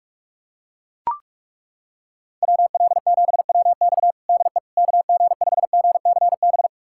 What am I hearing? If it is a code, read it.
Morse code, heard as a keyed tone, about 42 wpm, 700 Hz. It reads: WL7WX DE KG5GCB
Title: USA Callsign DE Callsign 45wpm